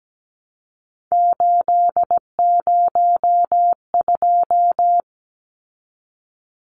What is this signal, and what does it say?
Morse code, heard as a keyed tone, about 17 wpm, 700 Hz. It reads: 802
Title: Numbers - 3-Digit 17wpm